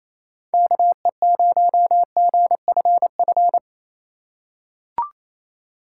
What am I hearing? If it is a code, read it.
Morse code, heard as a keyed tone, about 28 wpm, 700 Hz. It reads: KE0GFF
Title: USA Call Signs - 4x Character Spacing - 28wpm